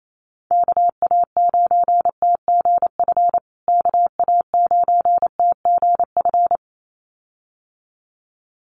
Morse code, heard as a keyed tone, about 28 wpm, 700 Hz. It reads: KA9TGF KA9TGF
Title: USA Call Signs Repeated 2x 28wpm